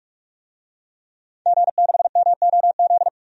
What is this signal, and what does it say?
Morse code, heard as a keyed tone, about 44 wpm, 700 Hz. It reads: G6MOZ
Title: DXCC-100 Call Signs - 5 Characters - 45wpm